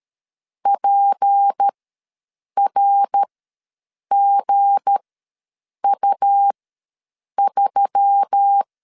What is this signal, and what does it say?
Morse code, heard as a keyed tone, about 13 wpm, spaced as for 7 wpm, 800 Hz. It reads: PRGU3
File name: LEZIONI / 7 GF9Z / 7 G ws 12wpm 3140